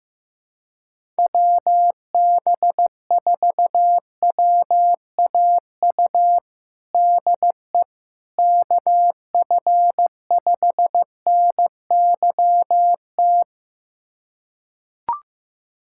Morse code, heard as a keyed tone, about 15 wpm, 700 Hz. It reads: WB4WAU DE KF5NYT
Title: USA Callsign DE Callsign - 6 Characters - 15wpm